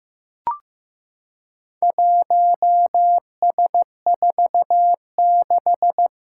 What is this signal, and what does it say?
Morse code, heard as a keyed tone, about 15 wpm, 700 Hz. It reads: 1S46
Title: Four Challenging Letter-Number Combinations 15wpm